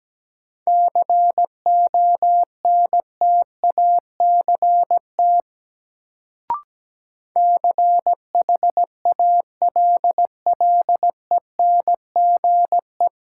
Morse code, heard as a keyed tone, about 17 wpm, 700 Hz. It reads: CONTACT CHALLENGE
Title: Sentences from Top 1000 Words - Review Part 7 - New Word Drill - 17wpm